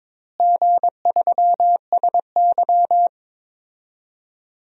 Morse code, heard as a keyed tone, about 22 wpm, 700 Hz. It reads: G3SY
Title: DXCC-100 Call Signs - 4 Characters - Repeated 2x - 22wpm